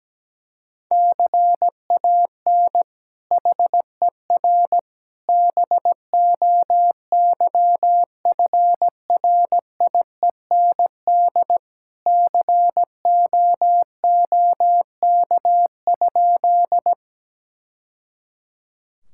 Morse code, heard as a keyed tone, about 17 wpm, 700 Hz. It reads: CAN HER BOYFRIEND COOK?
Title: Sentences from Top 1000 Words - Review Part 5 - 17wpm